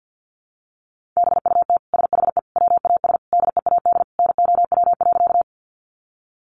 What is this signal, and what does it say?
Morse code, heard as a keyed tone, about 50 wpm, 700 Hz. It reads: BUT SHE WAS DEAD NOW.
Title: Sentences from Top 1500 Words 50wpm